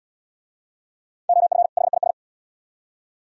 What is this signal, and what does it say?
Morse code, heard as a keyed tone, about 67 wpm, 700 Hz. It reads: OF HIS